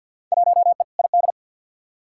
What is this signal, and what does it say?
Morse code, heard as a keyed tone, about 48 wpm, 700 Hz. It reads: 1E ID
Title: Field Day Exchanges - 50wpm